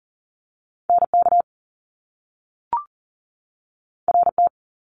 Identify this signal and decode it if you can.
Morse code, heard as a keyed tone, about 42 wpm, 700 Hz. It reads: NK RT
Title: Top Two Letter Combinations in Words 40wpm